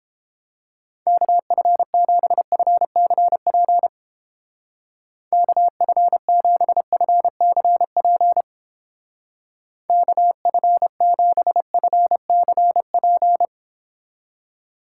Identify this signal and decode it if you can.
Morse code, heard as a keyed tone, about 30 wpm, 700 Hz. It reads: KF7FCP KF7FCP KF7FCP